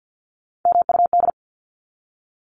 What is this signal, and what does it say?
Morse code, heard as a keyed tone, about 50 wpm, 700 Hz. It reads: MUD
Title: Top 1751-2000 Words 50wpm